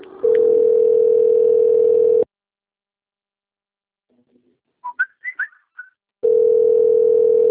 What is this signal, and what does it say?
Ringback tone (North America), 440 + 480 Hz.
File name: rg-1600-unknown- - - .8517